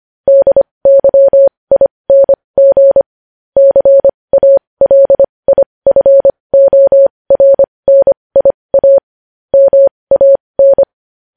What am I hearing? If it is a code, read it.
Morse code, heard as a keyed tone, about 25 wpm, spaced as for 20 wpm, 550 Hz. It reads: DYING CALIFORNIA MAN
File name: podcast / News headlines-1x-25x20